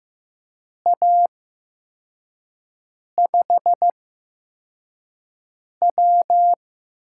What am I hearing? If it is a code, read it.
Morse code, heard as a keyed tone, about 15 wpm, spaced as for 4 wpm, 700 Hz. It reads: A5W
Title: USA Call Signs - 6 Characters - 8x Character Spacing - 15wpm